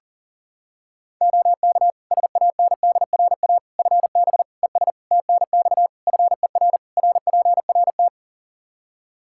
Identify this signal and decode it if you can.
Morse code, heard as a keyed tone, about 40 wpm, 700 Hz. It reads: OK SANDRA FB ES TNX FER RPRT
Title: Practice QSO - Exchange 2 - Info - 40wpm